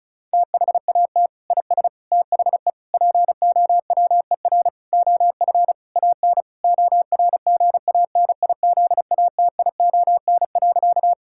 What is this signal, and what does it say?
Morse code, heard as a keyed tone, about 34 wpm, 700 Hz. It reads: THAT IS THE POWER OF AN ORGANIZATION.
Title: Sentences from Top 1000 Words - Review Part 5 - 35wpm